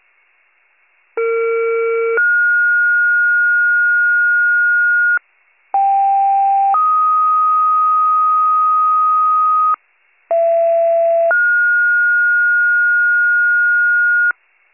Two-tone page, heard as a one-second tone, then a three-second tone, about 460 then 1430 Hz, about 790 then 1250 Hz, about 670 then 1430 Hz, with noise. It is heard from 1.2 to 14.3 s.